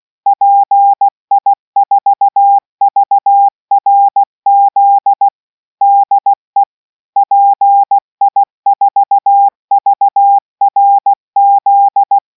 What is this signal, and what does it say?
Morse code, heard as a keyed tone, about 16 wpm, 800 Hz. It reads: PI4VRZ DE PI4VRZ